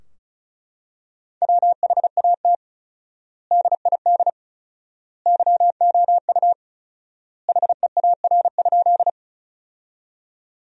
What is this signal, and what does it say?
Morse code, heard as a keyed tone, about 34 wpm, 700 Hz. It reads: WHAT DID YOU HEAR?
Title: 4x Word-Spacing - Sentences from Top 300 Words - 35wpm